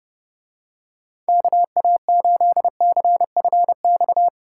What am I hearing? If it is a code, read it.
Morse code, heard as a keyed tone, about 30 wpm, 700 Hz. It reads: KA8CFX